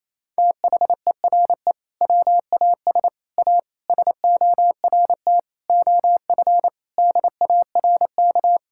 Morse code, heard as a keyed tone, about 28 wpm, 700 Hz. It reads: THERE WAS A SORT OF DARK
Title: Sentences from Top 1600 Words 28wpm